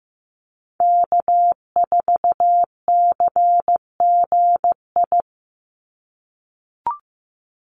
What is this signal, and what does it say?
Morse code, heard as a keyed tone, about 15 wpm, 700 Hz. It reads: K4CGI
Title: USA Call Signs 15wpm